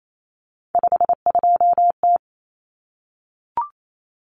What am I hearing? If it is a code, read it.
Morse code, heard as a keyed tone, about 28 wpm, 700 Hz. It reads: 52T
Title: Common Cut Numbers - 3-Digit 28wpm